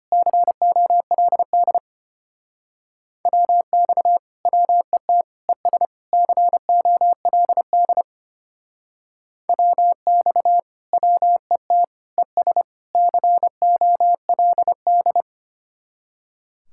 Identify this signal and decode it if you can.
Morse code, heard as a keyed tone, about 31 wpm, 700 Hz. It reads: COLD WX WET ES COLD WX WET ES COLD